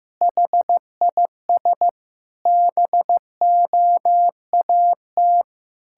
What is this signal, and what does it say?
Morse code, heard as a keyed tone, about 15 wpm, 700 Hz. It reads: HIS BOAT
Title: Pangram Challenge v2 - 15wpm